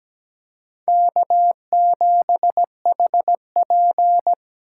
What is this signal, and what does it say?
Morse code, heard as a keyed tone, about 17 wpm, 700 Hz. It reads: K7HP
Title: USA Call Signs - 4 Characters - 2x Character Spacing - 17wpm